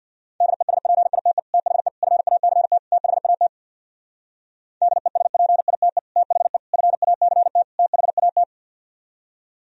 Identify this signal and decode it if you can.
Morse code, heard as a keyed tone, about 58 wpm, 700 Hz. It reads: DESPITE THE FACT THAT DESPITE THE FACT THAT
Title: Speed-Racing - Sets of 4 Words 50wpm